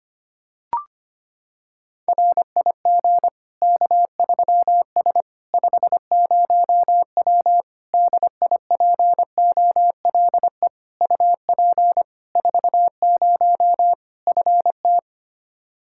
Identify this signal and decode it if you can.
Morse code, heard as a keyed tone, about 25 wpm, 700 Hz. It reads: RIG K3S 50W DIPOLE UP 40 FT
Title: K1USN SST QSO Practice 25wpm